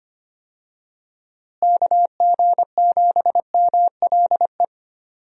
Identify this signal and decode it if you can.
Morse code, heard as a keyed tone, about 25 wpm, 700 Hz. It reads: KG7MLE
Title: USA Call Signs - 6 Characters - Repeated 2x - 25wpm